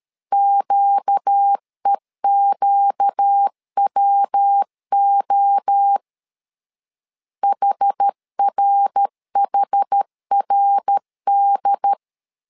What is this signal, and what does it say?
Morse code, heard as a keyed tone, about 13 wpm, 800 Hz. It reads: QEQWO HRHRD
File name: LEZIONI / 4 Q 12wpm 1833